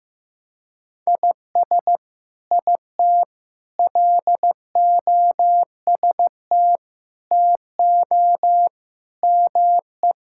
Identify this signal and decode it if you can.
Morse code, heard as a keyed tone, about 16 wpm, 700 Hz. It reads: IS IT LOST TO ME
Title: Sentences from Top 1600 Words 15wpm